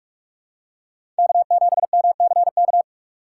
Morse code, heard as a keyed tone, about 46 wpm, 700 Hz. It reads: K7MCK
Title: USA Call Signs - 5 Characters - 6x Character Spacing - 45wpm